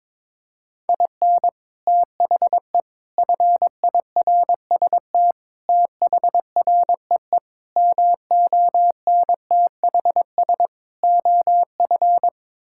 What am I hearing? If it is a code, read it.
Morse code, heard as a keyed tone, about 21 wpm, 700 Hz. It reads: IN THE FIRST THREE MONTHS OF
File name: practice / 22wpm / Sets of 6 Words - 22wpm